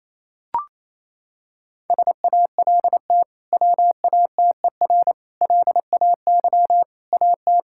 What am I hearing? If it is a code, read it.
Morse code, heard as a keyed tone, about 28 wpm, 700 Hz. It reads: SALT WATER LAY AT
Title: Sentences from Top 1000 Words - Review Part 5 - 28wpm